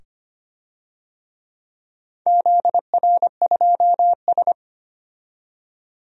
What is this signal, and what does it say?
Morse code, heard as a keyed tone, about 25 wpm, 700 Hz. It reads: ZR2S
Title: DXCC-100 Call Signs - 4 Characters - 25wpm